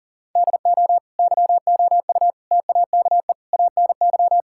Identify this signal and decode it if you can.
Morse code, heard as a keyed tone, about 40 wpm, 700 Hz. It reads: DO YOU TAKE ANY
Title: Sentences from Top 1000 Words - Review Part 6 - 40wpm